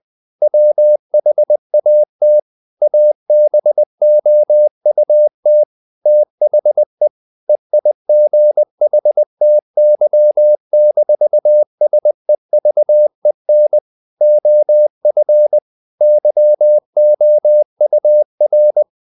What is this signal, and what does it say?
Morse code, heard as a keyed tone, about 20 wpm, 600 Hz. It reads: WHAT ABOUT THE EIGHTY-SEVEN OF YOUR
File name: the old man and the sea / the old man and the sea part 1 20wpm